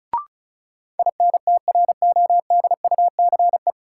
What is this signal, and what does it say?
Morse code, heard as a keyed tone, about 34 wpm, 700 Hz. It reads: INTRODUCE
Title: Sentences from Top 1600 Words - New Word Drill Alt - 35wpm